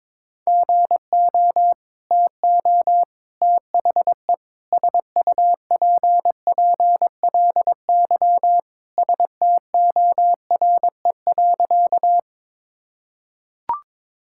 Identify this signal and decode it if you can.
Morse code, heard as a keyed tone, about 22 wpm, 700 Hz. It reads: GO TO THE SUPPLY STORE.
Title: Sentences from Top 1000 Words - Review Part 5 - 22wpm